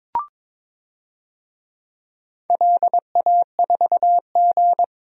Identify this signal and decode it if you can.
Morse code, heard as a keyed tone, about 22 wpm, 700 Hz. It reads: LA4G
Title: DXCC-100 Call Signs - 4 Characters - Repeated 2x - 22wpm